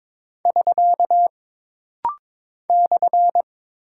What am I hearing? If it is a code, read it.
Morse code, heard as a keyed tone, about 22 wpm, 700 Hz. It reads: <SK> /
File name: practice / 22wpm / Single Punctuation-Prosign - Rapid-Fire - 22wpm